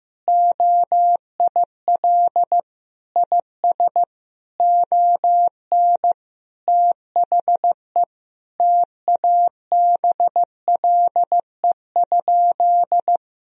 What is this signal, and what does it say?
Morse code, heard as a keyed tone, about 15 wpm, 700 Hz. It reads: OIL IS ON THE TABLE?
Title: Sentences from Top 1000 Words - Review Part 4 - 15wpm